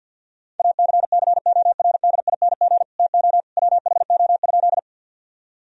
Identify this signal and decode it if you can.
Morse code, heard as a keyed tone, about 50 wpm, 700 Hz. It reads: ACCORDING TO WHO?
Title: Sentences from Top 1000 Words - Review Part 4 - 50wpm